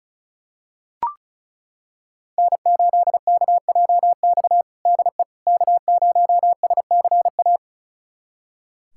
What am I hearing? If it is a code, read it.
Morse code, heard as a keyed tone, about 36 wpm, 700 Hz. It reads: N8KJX DE K0SCA
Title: USA Callsign DE Callsign - 5 Characters - 35wpm